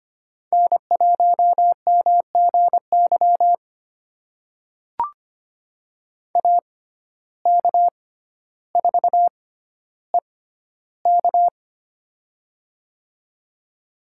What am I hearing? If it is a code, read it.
Morse code, heard as a keyed tone, about 25 wpm, spaced as for 13 wpm, 700 Hz. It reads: N1MGY AK4EK
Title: USA Call Signs - 5 Characters - 6x Character Spacing - 25wpm